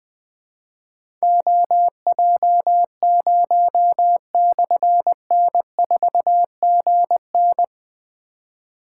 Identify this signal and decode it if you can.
Morse code, heard as a keyed tone, about 20 wpm, 700 Hz. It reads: OJ0/N4GN
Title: Difficult Call Signs 20wpm